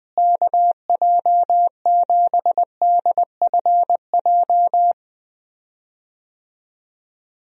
Morse code, heard as a keyed tone, about 20 wpm, 700 Hz. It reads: KJ7DFJ
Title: USA Call Signs - 6 Characters - Repeated 2x - 20wpm